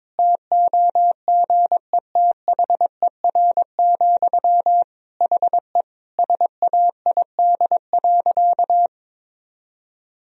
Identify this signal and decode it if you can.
Morse code, heard as a keyed tone, about 22 wpm, 700 Hz. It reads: TOGETHER, HE SAID.
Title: Sentences from Top 1000 Words - Review Part 2 - 22wpm